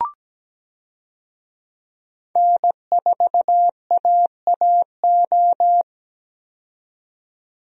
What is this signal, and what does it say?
Morse code, heard as a keyed tone, about 17 wpm, 700 Hz. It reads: N4AAO
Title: USA Call Signs - 5 Characters - 17wpm